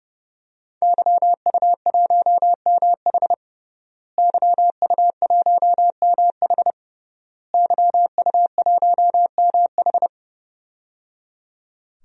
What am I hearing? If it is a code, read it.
Morse code, heard as a keyed tone, about 30 wpm, 700 Hz. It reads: YU1MH YU1MH YU1MH
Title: DXCC-100 Call Signs - 5 Characters - Repeated 3x - 30wpm